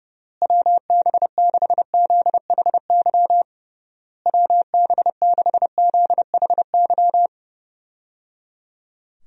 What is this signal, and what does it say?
Morse code, heard as a keyed tone, about 30 wpm, 700 Hz. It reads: WB6ZHY WB6ZHY
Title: USA Call Signs - 6 Characters - Repeated 3x - 30wpm